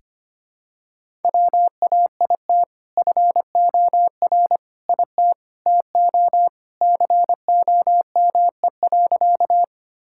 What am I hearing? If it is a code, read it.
Morse code, heard as a keyed tone, about 25 wpm, 700 Hz. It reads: WAIT FOR IT TO COME.
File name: practice / 25wpm / Sentences with 2-Word Phrases - 25wpm